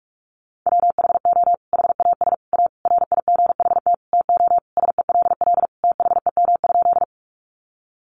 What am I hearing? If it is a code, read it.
Morse code, heard as a keyed tone, about 44 wpm, 700 Hz. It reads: WHO HAS A RIGHT TO SELL THEM?
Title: Sentences from Top 1100 Words 45wpm